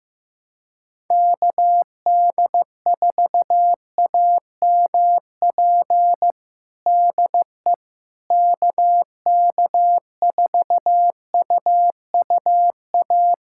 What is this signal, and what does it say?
Morse code, heard as a keyed tone, about 15 wpm, 700 Hz. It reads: KD4AMP DE KK4UUA
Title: USA Callsign DE Callsign 15wpm